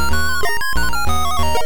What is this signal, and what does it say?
There is no code.